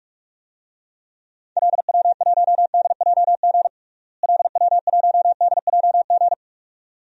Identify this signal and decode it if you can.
Morse code, heard as a keyed tone, about 46 wpm, 700 Hz. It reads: LW1DJG LW1DJG
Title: DXCC-100 Call Signs - 6 Characters - Repeated 2x - 45wpm